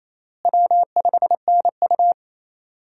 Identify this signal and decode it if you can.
Morse code, heard as a keyed tone, about 28 wpm, 700 Hz. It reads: W5NU
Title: USA Call Signs - 4 Characters - Repeated 2x - 28wpm